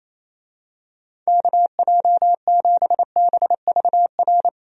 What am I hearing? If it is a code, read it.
Morse code, heard as a keyed tone, about 28 wpm, 700 Hz. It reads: KJ7BVR
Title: USA Call Signs - 6 Characters - 28wpm